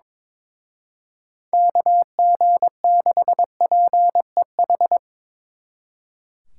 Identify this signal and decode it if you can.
Morse code, heard as a keyed tone, about 25 wpm, 700 Hz. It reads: KG6PEH
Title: Speed Racing - USA Call Signs 22wpm